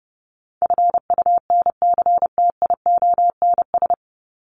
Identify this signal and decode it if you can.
Morse code, heard as a keyed tone, about 30 wpm, 700 Hz. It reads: FUNCTIONS
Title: ES:S Suffix - Top 100 Words with Base Words 30wpm